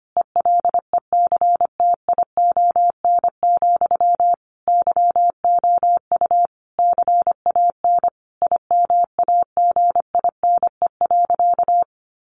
Morse code, heard as a keyed tone, about 25 wpm, 700 Hz. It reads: ELECTION, YOU CAN IMAGINE.
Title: Sentences from Top 1500 Words 25wpm